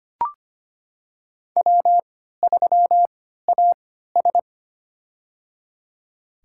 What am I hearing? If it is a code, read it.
Morse code, heard as a keyed tone, about 25 wpm, spaced as for 14 wpm, 700 Hz. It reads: W3AS
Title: USA Call Signs - 4 Characters - 3x Character Spacing - 25wpm